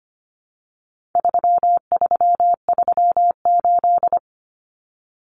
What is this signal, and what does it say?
Morse code, heard as a keyed tone, about 25 wpm, 700 Hz. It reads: 3338
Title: Common Cut Numbers - 4-Digit 25wpm